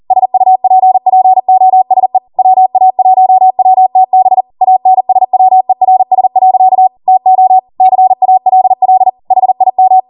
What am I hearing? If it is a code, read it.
Morse code, heard as a keyed tone, about 40 wpm, 750 Hz. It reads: SUPPOSE WA1WTB ANSWERS. TO CALL HIM